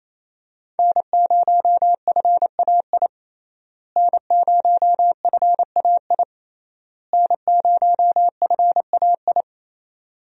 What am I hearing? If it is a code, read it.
Morse code, heard as a keyed tone, about 28 wpm, 700 Hz. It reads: N0FAI N0FAI N0FAI
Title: USA Call Signs - 5 Characters - Repeated 3x - 28wpm